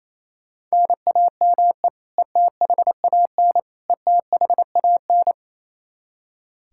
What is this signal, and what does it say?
Morse code, heard as a keyed tone, about 27 wpm, 700 Hz. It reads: NAME ETHAN ETHAN
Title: Practice QSO - Exchange 1 - Beginning - 28wpm